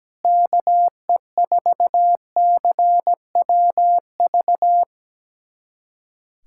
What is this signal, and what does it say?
Morse code, heard as a keyed tone, about 17 wpm, 700 Hz. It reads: KE4CWV
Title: USA Callsign DE Callsign 17wpm